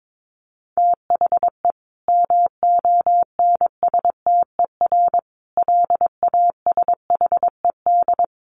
Morse code, heard as a keyed tone, about 22 wpm, 700 Hz. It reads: THE MONSTER LASHED
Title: A Sound Of Thunder 22wpm